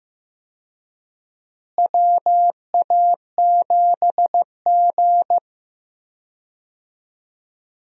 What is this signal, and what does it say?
Morse code, heard as a keyed tone, about 15 wpm, 700 Hz. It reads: WA7G
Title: USA Call Signs - 4 Characters - Repeated 2x - 15wpm